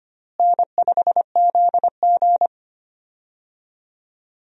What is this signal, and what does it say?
Morse code, heard as a keyed tone, about 25 wpm, 700 Hz. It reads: N5ZG
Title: USA Call Signs - 4 Characters - Repeated 2x - 25wpm